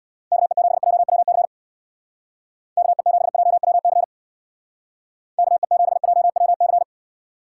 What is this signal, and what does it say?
Morse code, heard as a keyed tone, about 60 wpm, 700 Hz. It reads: KE7JWC KE7JWC KE7JWC
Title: Speed Racing - USA 2x3 Call Signs 50wpm